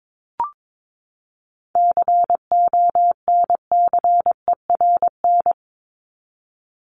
Morse code, heard as a keyed tone, about 22 wpm, 700 Hz. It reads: CONCERN
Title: Top 751-1000 Words 22wpm